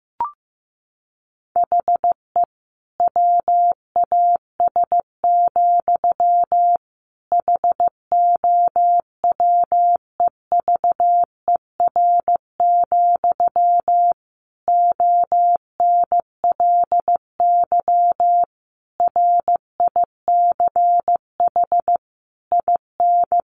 Morse code, heard as a keyed tone, about 15 wpm, 700 Hz. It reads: HE WAS, HOWEVER, ONLY RICH IN
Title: Sentences from Top 1100 Words 15wpm